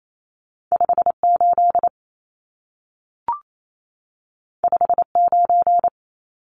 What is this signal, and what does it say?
Morse code, heard as a keyed tone, about 28 wpm, 700 Hz. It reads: 58 59
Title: Numbers - 2-Digit 28wpm